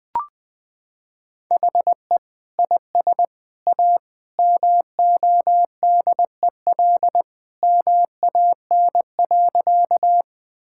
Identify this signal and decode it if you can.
Morse code, heard as a keyed tone, about 20 wpm, 700 Hz. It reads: HE IS A MODEL MAN.
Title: Sentences from Top 1000 Words - Review Part 5 - 20wpm